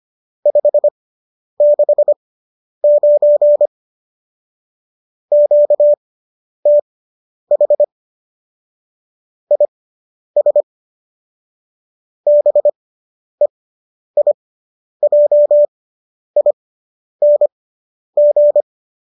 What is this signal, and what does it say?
Morse code, heard as a keyed tone, about 25 wpm, spaced as for 10 wpm, 590 Hz. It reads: 569 QTH IS BEIJING